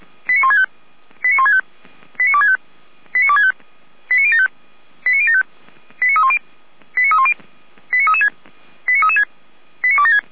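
Five-tone selective call (ZVEI), heard as quick runs of five tones, tones of about 70 ms. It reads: 89165 89165 89265 89265 89075 89075 89310 89310 89336 89336 89276
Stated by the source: ZVEI five-tone selective call 89078 received by SDR.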